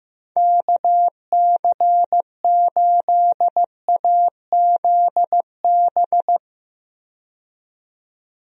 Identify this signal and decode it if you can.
Morse code, heard as a keyed tone, about 15 wpm, 700 Hz. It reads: KC8AZB
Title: USA Call Signs - 6 Characters - 15wpm